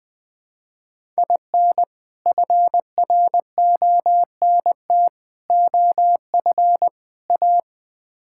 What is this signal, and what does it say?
Morse code, heard as a keyed tone, about 20 wpm, 700 Hz. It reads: IN FRONT OF A
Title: Build A Phrase - 4-Word Phrases - 20wpm